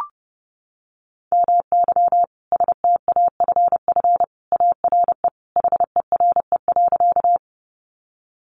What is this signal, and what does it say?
Morse code, heard as a keyed tone, about 30 wpm, 700 Hz. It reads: MY STAFF ARE HERE.
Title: Sentences from Top 1200 Words 30wpm